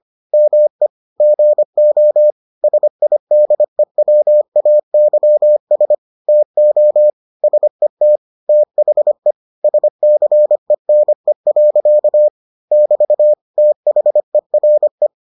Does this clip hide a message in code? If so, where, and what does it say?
Yes: Morse code from 0.3 to 15.1 s, heard as a keyed tone, about 25 wpm, 600 Hz. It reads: ME GO SIDEWAYS TO SET THE SCENE. <BT> THERE